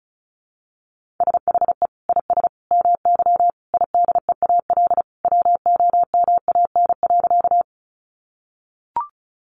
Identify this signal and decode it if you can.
Morse code, heard as a keyed tone, about 34 wpm, 700 Hz. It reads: SHE IS MY IDEAL WOMAN.
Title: Sentences from Top 1400 Words 35wpm